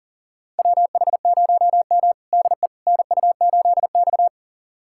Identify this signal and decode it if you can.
Morse code, heard as a keyed tone, about 40 wpm, 700 Hz. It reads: WH0M DE NU8X
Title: USA Callsign DE Callsign - 4 Characters - 40wpm